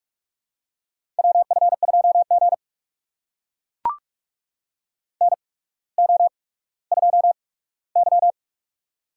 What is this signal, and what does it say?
Morse code, heard as a keyed tone, about 44 wpm, spaced as for 19 wpm, 690 Hz. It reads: WF2G NO2Y
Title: USA Call Signs - 4 Characters - 8x Character Spacing - 45wpm